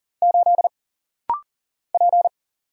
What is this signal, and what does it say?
Morse code, heard as a keyed tone, about 40 wpm, 700 Hz. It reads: 8 P
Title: Single Letter-Number Alt - Rapid-Fire 40wpm